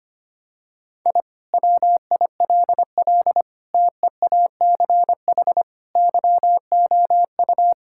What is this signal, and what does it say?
Morse code, heard as a keyed tone, about 25 wpm, 700 Hz. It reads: I WILL TEACH YOU
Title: Sentences from Top 1000 Words - Review Part 6 - 25wpm